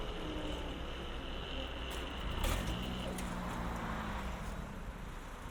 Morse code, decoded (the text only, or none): none